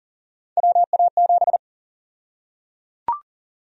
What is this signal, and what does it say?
Morse code, heard as a keyed tone, about 37 wpm, 700 Hz. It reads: WA7
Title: USA Call Sign Prefixes - Repeated 2x - 40wpm